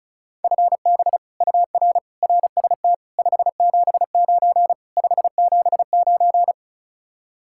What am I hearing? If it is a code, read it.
Morse code, heard as a keyed tone, about 44 wpm, 700 Hz. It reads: FB UR RST 579 579